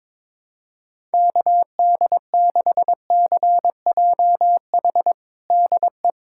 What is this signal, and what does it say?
Morse code, heard as a keyed tone, about 21 wpm, 700 Hz. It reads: KD6CJH DE
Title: USA Callsign DE Callsign 22wpm